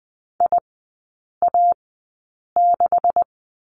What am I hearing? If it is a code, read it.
Morse code, heard as a keyed tone, about 20 wpm, spaced as for 8 wpm, 700 Hz. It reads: IA6
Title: 191 Lesson 31e - Letter v - Three Characters 20wpm